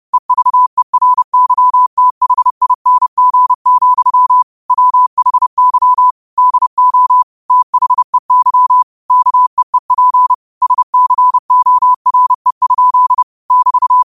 Morse code, heard as a keyed tone, about 30 wpm, 1000 Hz. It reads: EVERYTHING, WHY DO THEY KEEP SCORE? <BT>